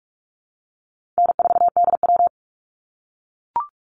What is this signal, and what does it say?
Morse code, heard as a keyed tone, about 44 wpm, 700 Hz. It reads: N4DW
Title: Call Signs - 45wpm